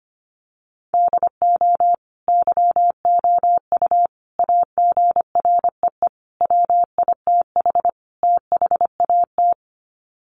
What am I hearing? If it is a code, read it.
Morse code, heard as a keyed tone, about 25 wpm, 700 Hz. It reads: DO YOU AGREE WITH THAT
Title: Sets of 5 Words Encore 25wpm